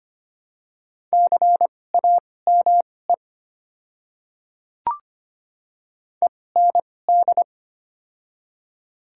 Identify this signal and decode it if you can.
Morse code, heard as a keyed tone, about 25 wpm, spaced as for 18 wpm, 700 Hz. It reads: CAME END
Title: 2x Character Spacing - Top 200 Words - 25wpm